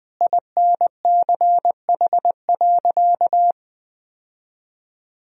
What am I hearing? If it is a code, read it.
Morse code, heard as a keyed tone, about 20 wpm, 700 Hz. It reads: INCH.
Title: Sentences from Top 1000 Words - Review Part 3 - 20wpm